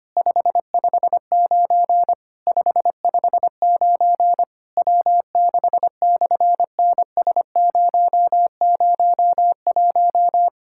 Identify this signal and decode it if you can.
Morse code, heard as a keyed tone, about 24 wpm, 700 Hz. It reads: 559 559 W6/NS001
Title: SOTA QSO Practice 25wpm